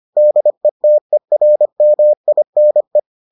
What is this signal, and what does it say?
Morse code, heard as a keyed tone, about 25 wpm, 600 Hz. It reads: DETERMINE